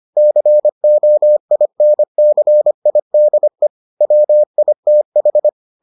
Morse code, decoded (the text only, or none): COINCIDE WITH